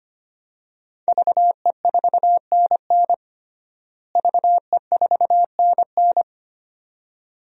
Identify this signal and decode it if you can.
Morse code, heard as a keyed tone, about 25 wpm, 700 Hz. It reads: VE4NN VE4NN
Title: DXCC-100 Call Signs - 5 Characters - Repeated 2x - 25wpm